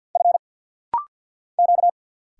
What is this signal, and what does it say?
Morse code, heard as a keyed tone, about 50 wpm, 700 Hz. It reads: F Q
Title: Single Letters - Rapid-Fire 50wpm